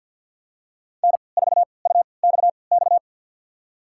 Morse code, heard as a keyed tone, about 51 wpm, spaced as for 28 wpm, 700 Hz. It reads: N4UXX
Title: USA Call Signs - 3x Character Spacing - 50wpm